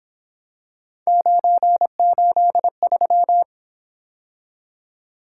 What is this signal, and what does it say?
Morse code, heard as a keyed tone, about 28 wpm, 700 Hz. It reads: 983